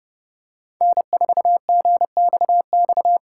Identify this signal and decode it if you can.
Morse code, heard as a keyed tone, about 29 wpm, 700 Hz. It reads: N4GXX